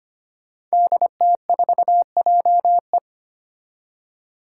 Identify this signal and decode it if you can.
Morse code, heard as a keyed tone, about 25 wpm, 700 Hz. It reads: DT4JE
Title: Five Challenging Letter-Number Combinations 25wpm